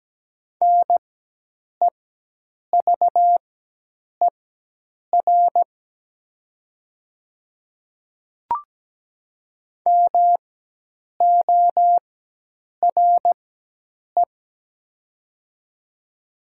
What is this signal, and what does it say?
Morse code, heard as a keyed tone, about 17 wpm, spaced as for 8 wpm, 700 Hz. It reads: NEVER MORE
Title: 4x Character Spacing - Top 500 Words - 17wpm